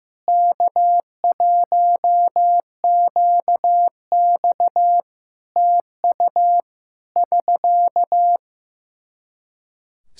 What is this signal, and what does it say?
Morse code, heard as a keyed tone, about 15 wpm, 700 Hz. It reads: K1QX TU <SK>
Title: Practice QSO - Exchange 4 - Ending - 15wpm